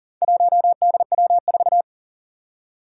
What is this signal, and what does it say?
Morse code, heard as a keyed tone, about 37 wpm, 700 Hz. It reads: 1DW4
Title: Four Challenging Letter-Number Combinations 40wpm